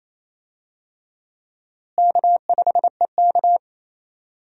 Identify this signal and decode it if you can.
Morse code, heard as a keyed tone, about 28 wpm, 700 Hz. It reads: K5EK
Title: USA Call Signs - 4 Characters - Repeated 3x - 28wpm